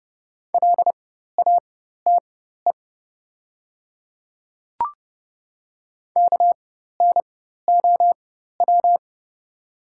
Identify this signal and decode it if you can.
Morse code, heard as a keyed tone, about 30 wpm, spaced as for 14 wpm, 700 Hz. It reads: LATE KNOW
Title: 4x Character Spacing - Top 300 Words - 30wpm